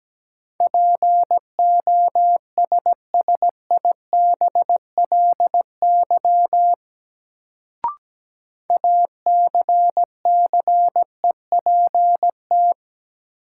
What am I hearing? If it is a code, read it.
Morse code, heard as a keyed tone, about 17 wpm, 700 Hz. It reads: POSSIBLY ACCEPT
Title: Sentences from Top 1300 Words - New Word Drill Alt - 17wpm